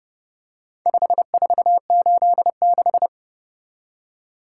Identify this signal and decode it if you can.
Morse code, heard as a keyed tone, about 29 wpm, 700 Hz. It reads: 5486